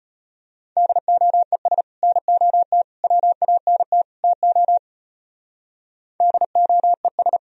Morse code, heard as a keyed tone, about 33 wpm, 700 Hz. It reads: DOES NOT WANT TO DOES